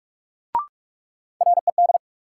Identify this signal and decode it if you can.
Morse code, heard as a keyed tone, about 46 wpm, 700 Hz. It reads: RED
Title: Sentences from Top 700 Words - New Word Drill - 45wpm